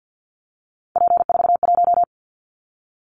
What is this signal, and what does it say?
Morse code, heard as a keyed tone, about 50 wpm, 700 Hz. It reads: P41
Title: Callsign Prefixes for Top 100 DXCC - 50wpm